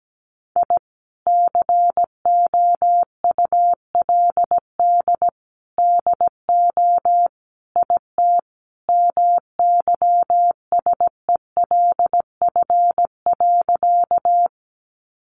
Morse code, heard as a keyed tone, about 17 wpm, 700 Hz. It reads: I COULD DO IT MYSELF.
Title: Sentences from Top 1300 Words 17wpm